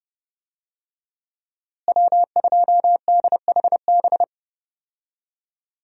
Morse code, heard as a keyed tone, about 30 wpm, 700 Hz. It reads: W2DHB
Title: USA Call Signs - 5 Characters - Repeated 3x - 30wpm